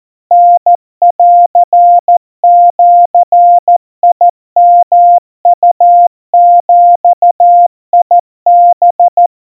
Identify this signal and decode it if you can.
Morse code (Japanese Wabun code), heard as a keyed tone, about 14 wpm, 700 Hz. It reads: タンジヨウビハ
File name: kawa / Morse sample / kamyu 50 cpm 700 Hz Crispy